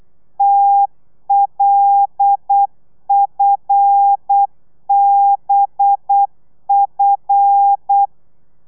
Morse code, heard as a keyed tone, about 8 wpm, 800 Hz. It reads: TLFBF
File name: telegrafia / i2tfj / Lez5 35